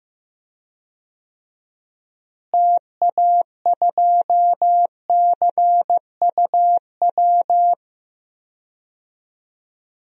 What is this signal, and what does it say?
Morse code, heard as a keyed tone, about 15 wpm, 700 Hz. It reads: TA2CUW
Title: DXCC-100 Call Signs - 6 Characters - Repeated 2x - 15wpm